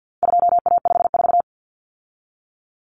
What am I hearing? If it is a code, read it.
Morse code, heard as a keyed tone, about 50 wpm, 700 Hz. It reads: 2A54
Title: Common Cut Numbers - 4-Digit 50wpm